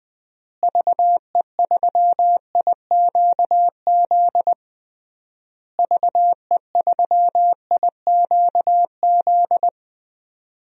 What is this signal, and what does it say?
Morse code, heard as a keyed tone, about 20 wpm, 700 Hz. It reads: VE3IQZ VE3IQZ
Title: DXCC-100 Call Signs - 6 Characters - Repeated 3x - 20wpm